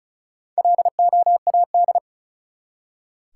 Morse code, heard as a keyed tone, about 36 wpm, 700 Hz. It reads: LOAD
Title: Four Letter Words - 35wpm